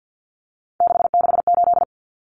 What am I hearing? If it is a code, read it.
Morse code, heard as a keyed tone, about 50 wpm, 700 Hz. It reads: 668
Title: Numbers - 3-Digit 50wpm